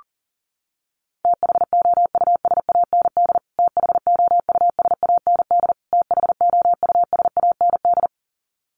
Morse code, heard as a keyed tone, about 40 wpm, 700 Hz. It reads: THOUSAND THOUSAND THOUSAND
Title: Top 500 Words Repeated 40wpm